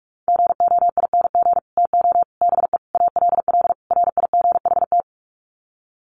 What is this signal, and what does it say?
Morse code, heard as a keyed tone, about 44 wpm, 700 Hz. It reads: GOING TO BE ALL RIGHT
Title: Sets of 5 Words Encore2 45wpm